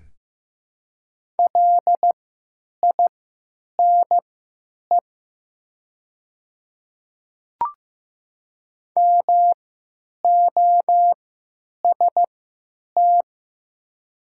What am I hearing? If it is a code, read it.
Morse code, heard as a keyed tone, about 15 wpm, spaced as for 9 wpm, 700 Hz. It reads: LINE MOST
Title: 3x Character Spacing - Top 200 Words - 15wpm